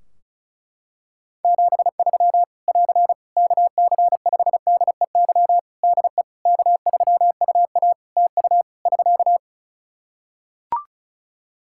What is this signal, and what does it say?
Morse code, heard as a keyed tone, about 36 wpm, 700 Hz. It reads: 73 <AR> KC5DEY DE K3UA TU <SK>
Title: Practice QSO - Exchange 4 - Ending - 35wpm